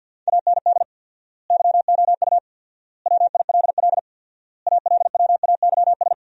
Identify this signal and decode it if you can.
Morse code, heard as a keyed tone, about 48 wpm, 700 Hz. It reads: AND YOU WILL ALWAYS
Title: 4x Word-Spacing - Sentences from Top 400 Words - 50wpm